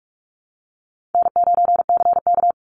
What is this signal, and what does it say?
Morse code, heard as a keyed tone, about 44 wpm, 700 Hz. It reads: N9CK
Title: Easy Call Signs Repeated 2x - 45wpm